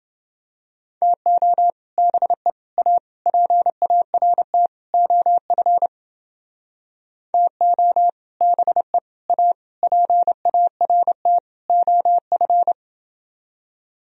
Morse code, heard as a keyed tone, about 29 wpm, 700 Hz. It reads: TO BE A PART OF TO BE A PART OF